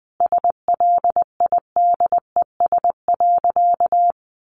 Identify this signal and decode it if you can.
Morse code, heard as a keyed tone, about 19 wpm, 700 Hz. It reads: SLIDES.
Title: Easy Phrases 20wpm